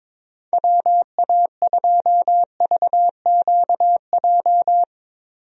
Morse code, heard as a keyed tone, about 21 wpm, 700 Hz. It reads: WA2VQJ
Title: USA Call Signs - 6 Characters - 22wpm